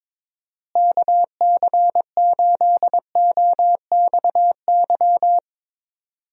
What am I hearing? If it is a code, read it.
Morse code, heard as a keyed tone, about 22 wpm, 700 Hz. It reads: KC8OXY
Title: USA Call Signs - 6 Characters - Repeated 3x - 22wpm